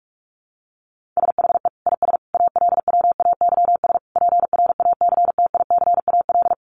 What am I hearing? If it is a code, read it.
Morse code, heard as a keyed tone, about 44 wpm, 700 Hz. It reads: SHE IS ALWAYS PRACTICAL
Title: Sentences from Top 1400 Words 45wpm